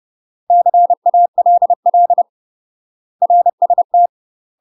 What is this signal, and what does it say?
Morse code, heard as a keyed tone, about 30 wpm, 700 Hz. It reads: CALL RST
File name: UK MASTER QSOs TRAINING/30WPM